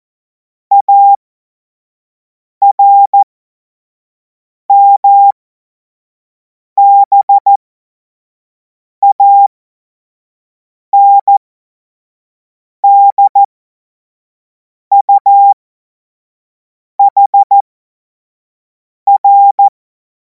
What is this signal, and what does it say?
Morse code, heard as a keyed tone, about 14 wpm, spaced as for 5 wpm, 800 Hz. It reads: ARMBANDUHR
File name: Morsekurs / Witze / Witz 17. Februar 2026 25 F70